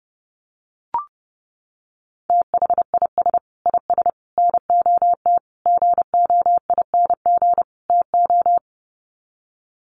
Morse code, heard as a keyed tone, about 30 wpm, 700 Hz. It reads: THIS IS NOT GOING TO
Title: Sets of 5 Words Encore2 30wpm